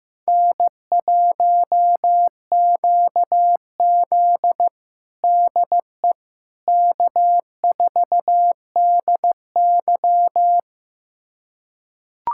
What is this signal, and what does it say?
Morse code, heard as a keyed tone, about 15 wpm, 700 Hz. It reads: N1QZ DE K4DY
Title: USA Callsign DE Callsign - 4 Characters - 15wpm